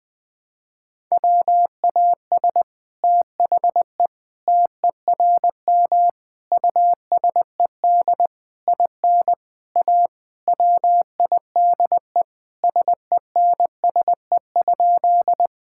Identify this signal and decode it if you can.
Morse code, heard as a keyed tone, about 21 wpm, 700 Hz. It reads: WAS THE TERM USED IN A WIDE SENSE?
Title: Sentences from Top 1000 Words - Review Part 6 - 20wpm